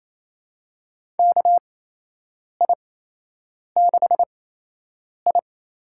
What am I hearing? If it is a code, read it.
Morse code, heard as a keyed tone, about 28 wpm, spaced as for 8 wpm, 700 Hz. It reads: KI6I
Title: USA Call Signs - 6 Characters - 8x Character Spacing - 28wpm